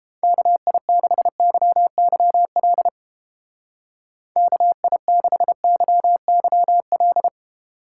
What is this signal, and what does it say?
Morse code, heard as a keyed tone, about 29 wpm, 700 Hz. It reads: KI6YYL KI6YYL